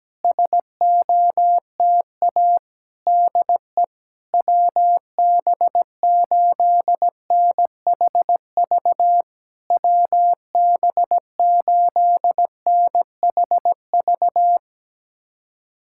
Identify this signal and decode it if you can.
Morse code, heard as a keyed tone, about 17 wpm, 700 Hz. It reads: SOTA DE WB8NHV WB8NHV
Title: SOTA QSO Practice 17wpm